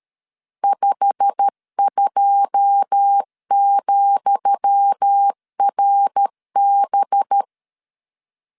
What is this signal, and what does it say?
Morse code, heard as a keyed tone, about 13 wpm, 800 Hz. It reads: 52,RB
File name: LEZIONI / 10 SIGNS / 10D COMA 12wpm 2544